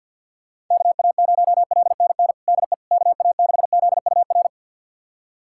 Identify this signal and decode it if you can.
Morse code, heard as a keyed tone, about 53 wpm, 700 Hz. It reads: KA0LNN DE KA6ZUR
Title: USA Callsign DE Callsign - 6 Characters - 50wpm